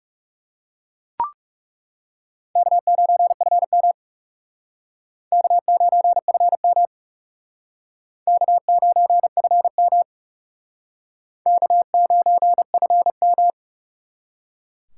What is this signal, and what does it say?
Morse code, heard as a keyed tone, about 38 wpm, 700 Hz. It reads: K9FM K9FM K9FM K9FM